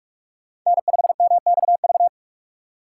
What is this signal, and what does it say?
Morse code, heard as a keyed tone, about 55 wpm, 700 Hz. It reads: N5MXV
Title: Speed Racing - USA 1x3 Call Signs 45wpm